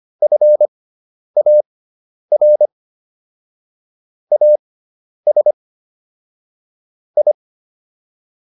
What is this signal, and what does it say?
Morse code, heard as a keyed tone, about 25 wpm, spaced as for 10 wpm, 600 Hz. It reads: FAR AS I